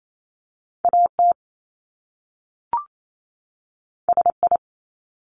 Morse code, heard as a keyed tone, about 28 wpm, 700 Hz. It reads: AT SI
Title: Top Two Letter Combinations in Words 28wpm